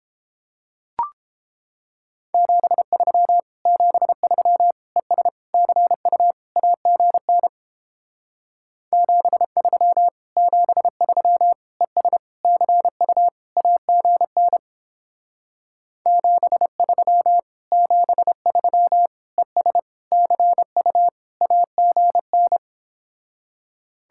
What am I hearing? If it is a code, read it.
Morse code, heard as a keyed tone, about 29 wpm, 700 Hz. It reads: 73 73 ES CU AGN 73 73 ES CU AGN 73 73 ES CU AGN